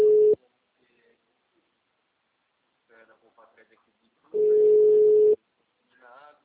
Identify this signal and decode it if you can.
Ringback tone (Europe, 425 Hz), with speech.